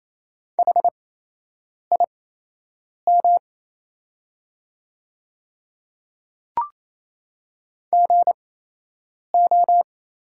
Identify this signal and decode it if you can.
Morse code, heard as a keyed tone, about 28 wpm, spaced as for 8 wpm, 700 Hz. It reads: HIM GO
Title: 8x Character Spacing - Top 100 Words - 28wpm